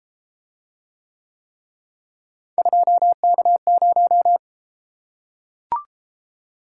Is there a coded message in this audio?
Yes: Morse code from 2.6 to 4.4 s, heard as a keyed tone, about 32 wpm, 700 Hz. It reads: 2K0